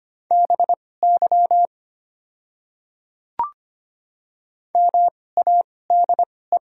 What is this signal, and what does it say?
Morse code, heard as a keyed tone, about 25 wpm, spaced as for 18 wpm, 700 Hz. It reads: BY MADE
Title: 2x Character Spacing - Top 200 Words - 25wpm